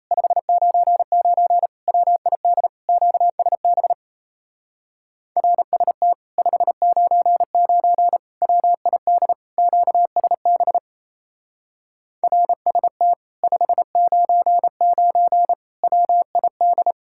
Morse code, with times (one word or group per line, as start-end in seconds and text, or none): 0.1-1.7 599
1.9-2.7 WID
2.9-3.9 QSB
5.4-6.1 RST
6.4-8.2 599
8.4-9.3 WID
9.6-10.8 QSB
12.2-13.1 RST
13.4-15.5 599
15.8-16.9 WID